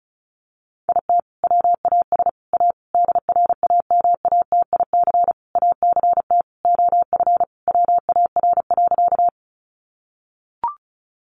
Morse code, heard as a keyed tone, about 34 wpm, 700 Hz. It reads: IT WAS A DRAMATIC ACT OF WAR.
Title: Sentences from Top 1300 Words 35wpm